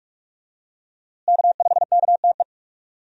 Morse code, heard as a keyed tone, about 46 wpm, 700 Hz. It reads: K5KTE
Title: USA Call Signs - 5 Characters - Repeated 3x - 45wpm